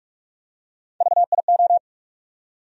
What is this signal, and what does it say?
Morse code, heard as a keyed tone, about 46 wpm, 700 Hz. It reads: VIO
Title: USA Call Sign Postfixes - 3 Letters 45wpm